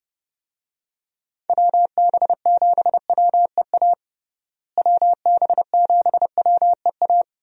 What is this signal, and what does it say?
Morse code, heard as a keyed tone, about 29 wpm, 700 Hz. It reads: WB7WEA WB7WEA
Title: USA Call Signs - 6 Characters - Repeated 2x - 30wpm